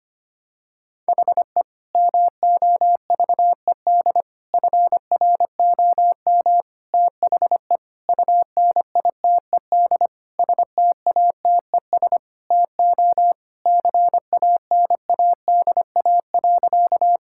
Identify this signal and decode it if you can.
Morse code, heard as a keyed tone, about 25 wpm, 700 Hz. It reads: HE MOVED FROM THE UNITED STATES TO CANADA.
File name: practice / 25wpm / Sentences with 4-Word Phrases - 25wpm